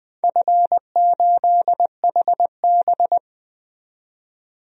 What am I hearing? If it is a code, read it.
Morse code, heard as a keyed tone, about 20 wpm, 700 Hz. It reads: F8HB
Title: DXCC-100 Call Signs - 4 Characters - 20wpm